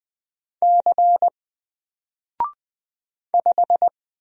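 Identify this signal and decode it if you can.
Morse code, heard as a keyed tone, about 20 wpm, 700 Hz. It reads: C 5
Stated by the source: Single letter and number Morse drill at 20 wpm, from N4LCD.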